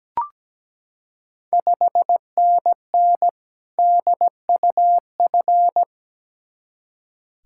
Morse code, heard as a keyed tone, about 17 wpm, 690 Hz. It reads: 5NN DUF
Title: Ontario QSO Party Exchange 17wpm